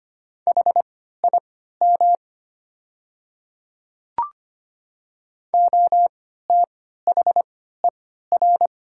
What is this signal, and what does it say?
Morse code, heard as a keyed tone, about 25 wpm, spaced as for 14 wpm, 700 Hz. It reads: HIM OTHER
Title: 3x Character Spacing - Top 100 Words - 25wpm